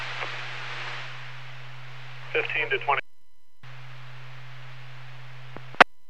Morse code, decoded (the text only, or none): none